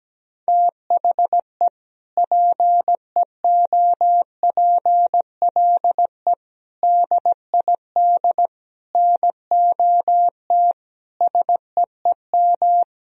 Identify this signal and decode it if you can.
Morse code, heard as a keyed tone, about 17 wpm, 700 Hz. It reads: THE PEOPLE DID NOT SEEM
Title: Sentences from Top 1000 Words - Review Part 1 - 17wpm